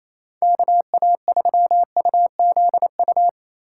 Morse code, heard as a keyed tone, about 35 wpm, 700 Hz. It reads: KA3UZU